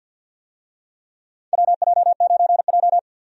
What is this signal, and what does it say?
Morse code, heard as a keyed tone, about 50 wpm, 700 Hz. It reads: WJ9J
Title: USA Call Signs - 4 Characters - 50wpm